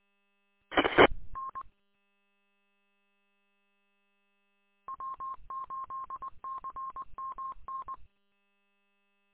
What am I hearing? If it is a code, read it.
Morse code, heard as a keyed tone, about 20 wpm, 1070 Hz. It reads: EE N W8CMN
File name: w8cmn- -combined